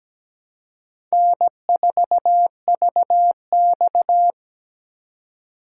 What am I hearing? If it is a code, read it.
Morse code, heard as a keyed tone, about 17 wpm, 700 Hz. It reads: N4VX
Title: USA Call Signs - 4 Characters - 3x Character Spacing - 17wpm